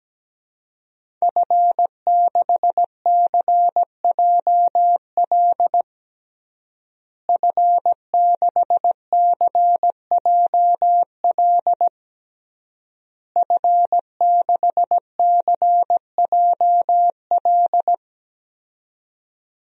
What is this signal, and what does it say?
Morse code, heard as a keyed tone, about 17 wpm, 700 Hz. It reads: F6CJL F6CJL F6CJL
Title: DXCC-100 Call Signs - 5 Characters - Repeated 3x - 17wpm